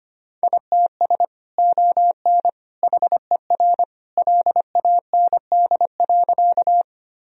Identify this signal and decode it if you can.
Morse code, heard as a keyed tone, about 25 wpm, 700 Hz. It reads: ITS ON HER LAND.
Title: Sentences from Top 1000 Words - Review Part 1 - 25wpm